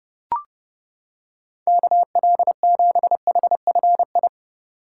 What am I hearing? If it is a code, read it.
Morse code, heard as a keyed tone, about 29 wpm, 700 Hz. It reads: KL7HFI
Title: K1USN SST QSO Practice - 30wpm